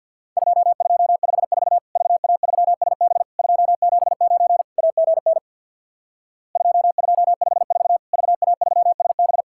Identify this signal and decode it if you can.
Morse code, heard as a keyed tone, about 50 wpm, 700 Hz. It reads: 2254 VA3SB 279 AGN 2254 VA3SB
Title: Worked All Europe DX Contest - QTC Exchange with Repeats - 50wpm